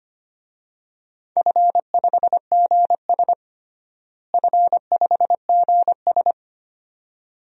Morse code, heard as a keyed tone, about 25 wpm, 700 Hz. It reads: F5GS F5GS
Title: DXCC-100 Call Signs - 4 Characters - Repeated 2x - 25wpm